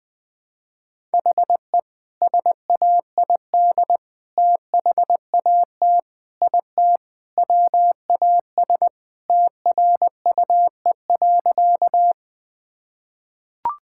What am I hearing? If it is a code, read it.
Morse code, heard as a keyed tone, about 20 wpm, 700 Hz. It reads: HE SAID THAT IT WAS TRUE.
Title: Sentences with 2-Word Phrases - 20wpm